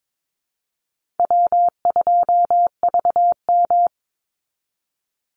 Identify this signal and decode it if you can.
Morse code, heard as a keyed tone, about 22 wpm, 700 Hz. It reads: W2VM
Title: Easy Call Signs Repeated 3x - 22wpm